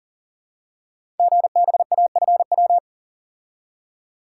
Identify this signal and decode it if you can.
Morse code, heard as a keyed tone, about 42 wpm, 690 Hz. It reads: GBAFW
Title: Code Groups - 5 Letters - 40wpm